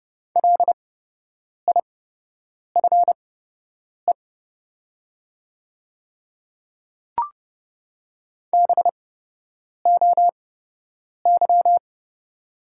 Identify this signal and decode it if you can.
Morse code, heard as a keyed tone, about 29 wpm, spaced as for 8 wpm, 700 Hz. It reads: LIFE BOY
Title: 8x Character Spacing - Top 500 Words - 30wpm